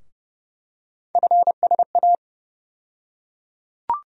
Morse code, heard as a keyed tone, about 29 wpm, 700 Hz. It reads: FSA
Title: USA Call Sign Postfixes - 3 Letters 30wpm